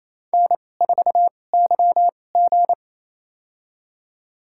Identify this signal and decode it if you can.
Morse code, heard as a keyed tone, about 28 wpm, spaced as for 20 wpm, 700 Hz. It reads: N4YG
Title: USA Call Signs - 4 Characters - 2x Character Spacing - 28wpm